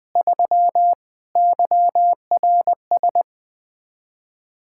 Morse code, heard as a keyed tone, about 20 wpm, 700 Hz. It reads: 3 YRS
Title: K1USN SST QSO Practice 20wpm